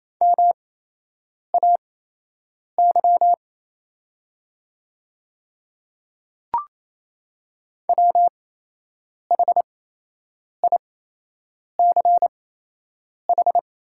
Morse code, heard as a keyed tone, about 28 wpm, spaced as for 8 wpm, 700 Hz. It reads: MAY WHICH
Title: 8x Character Spacing - Top 100 Words - 28wpm